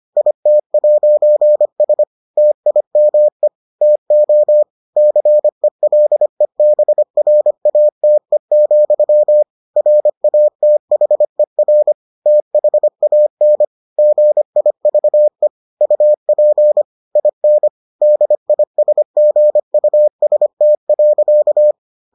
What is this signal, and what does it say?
Morse code, heard as a keyed tone, about 25 wpm, 600 Hz. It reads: IT'S TIME TO CELEBRATE, RATHER THAN GIVE UP IN DISGUST.